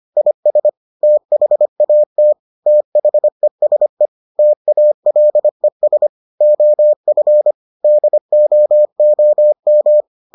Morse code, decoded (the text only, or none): IS THAT THESE TALES OF DOOM